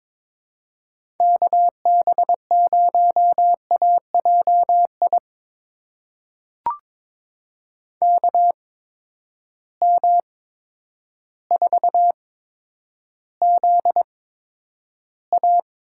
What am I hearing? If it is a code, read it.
Morse code, heard as a keyed tone, about 22 wpm, spaced as for 10 wpm, 700 Hz. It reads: KB0AJI KM4ZA
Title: USA Call Signs - 6 Characters - 8x Character Spacing - 22wpm